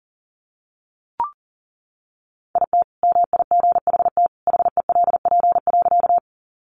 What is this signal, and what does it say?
Morse code, heard as a keyed tone, about 40 wpm, 700 Hz. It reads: IT MIGHT HELP.
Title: Sentences from Top 300 Words 40wpm